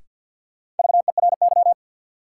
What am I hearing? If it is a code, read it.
Morse code, heard as a keyed tone, about 47 wpm, 700 Hz. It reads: VERY
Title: Sets of 2 Words - New Word Drill - 50wpm